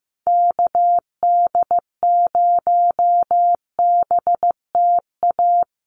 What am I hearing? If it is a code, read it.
Morse code, heard as a keyed tone, about 15 wpm, 700 Hz. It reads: KD0BTA
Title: USA Call Signs 15wpm